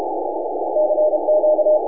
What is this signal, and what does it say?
Morse code, heard as a keyed tone, about 34 wpm, 610 Hz, with noise. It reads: CQ
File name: n6tr / N6AA Skimmed